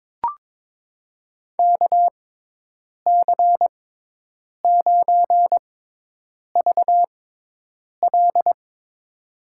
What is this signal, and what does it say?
Morse code, heard as a keyed tone, about 22 wpm, spaced as for 8 wpm, 700 Hz. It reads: KC9VL
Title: USA Call Signs - 6x Character Spacing - 22wpm